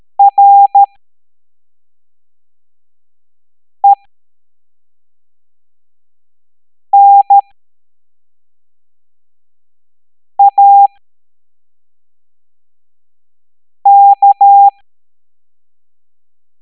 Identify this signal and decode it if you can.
Morse code, heard as a keyed tone, about 13 wpm, spaced as for 3 wpm, 800 Hz. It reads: RENAK
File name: hamdata / L01-EX02-ETANTK